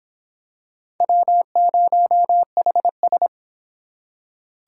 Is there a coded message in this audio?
Yes: Morse code from 1.0 to 3.3 s, heard as a keyed tone, about 24 wpm, 700 Hz. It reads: W0HS